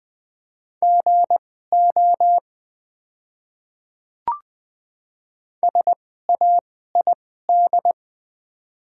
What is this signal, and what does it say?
Morse code, heard as a keyed tone, about 20 wpm, spaced as for 14 wpm, 700 Hz. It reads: GO SAID
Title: 2x Character Spacing - Top 500 Words - 20wpm